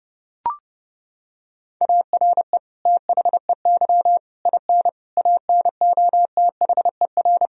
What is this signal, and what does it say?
Morse code, heard as a keyed tone, about 30 wpm, 700 Hz. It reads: ARE THEY IN ANOTHER
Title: Sentences from Top 1000 Words - Review Part 1 - 30wpm